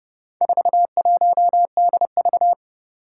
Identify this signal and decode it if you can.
Morse code, heard as a keyed tone, about 30 wpm, 700 Hz. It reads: 41DV
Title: Four Challenging Letter-Number Combinations 30wpm